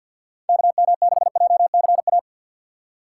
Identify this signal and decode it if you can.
Morse code, heard as a keyed tone, about 59 wpm, 700 Hz. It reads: KM6JCA
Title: Speed Racing - USA Call Signs 50wpm